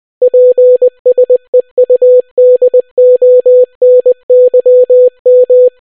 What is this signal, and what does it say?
Morse code, heard as a keyed tone, about 20 wpm, 500 Hz. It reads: PSEUDONYM